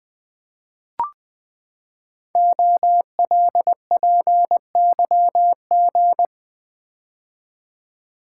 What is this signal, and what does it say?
Morse code, heard as a keyed tone, about 20 wpm, 700 Hz. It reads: OLPYG
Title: Code Groups - 5 Letters - 20wpm